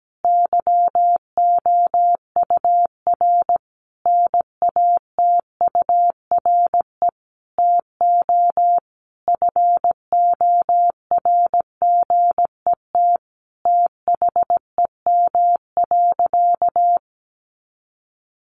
Morse code, heard as a keyed tone, about 17 wpm, 700 Hz. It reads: YOUR NATURE TO FORGET THEM.
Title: Sentences from Top 1200 Words 17wpm